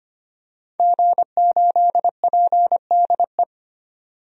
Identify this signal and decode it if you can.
Morse code, heard as a keyed tone, about 25 wpm, 700 Hz. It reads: G8PDE
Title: DXCC-100 Call Signs - 5 Characters - 25wpm